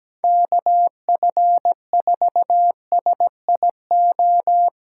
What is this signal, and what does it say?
Morse code, heard as a keyed tone, about 17 wpm, 700 Hz. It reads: KF4SIO
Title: USA Call Signs - 6 Characters - 6x Character Spacing - 17wpm